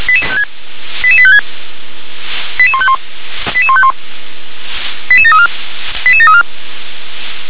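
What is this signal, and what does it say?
Five-tone selective call (ZVEI), heard as quick runs of five tones, tones of about 70 ms. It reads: 88856 88151 88151 88734 88734
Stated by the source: ZVEI five-tone selective call received with an SDR.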